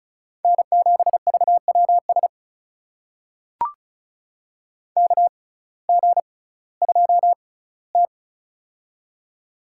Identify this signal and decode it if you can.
Morse code, heard as a keyed tone, about 36 wpm, spaced as for 19 wpm, 700 Hz. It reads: N7VWS KG2T
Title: USA Call Signs - 6x Character Spacing - 35wpm